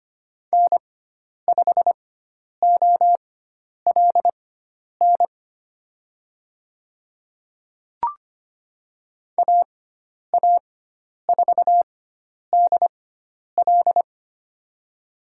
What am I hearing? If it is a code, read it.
Morse code, heard as a keyed tone, about 25 wpm, spaced as for 10 wpm, 700 Hz. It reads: N5OLN AA4DL
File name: Lesson 16g - Letter c - Call Signs 25wpm